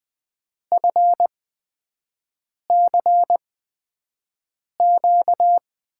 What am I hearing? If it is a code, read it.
Morse code, heard as a keyed tone, about 20 wpm, spaced as for 5 wpm, 700 Hz. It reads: FCQ
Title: USA Call Signs - 5 Characters - 8x Character Spacing - 20wpm